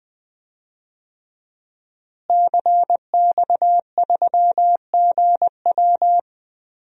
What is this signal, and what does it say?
Morse code, heard as a keyed tone, about 20 wpm, 700 Hz. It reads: CX3GW
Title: DXCC-100 Call Signs - 5 Characters - Repeated 2x - 20wpm